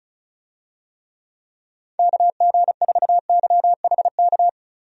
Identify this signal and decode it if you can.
Morse code, heard as a keyed tone, about 36 wpm, 700 Hz. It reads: KG4YHK
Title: USA Call Signs - 6 Characters - 35wpm